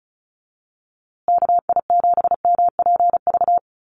Morse code, heard as a keyed tone, about 34 wpm, 700 Hz. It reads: KI7MPV
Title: USA Call Signs Repeated 3x 35wpm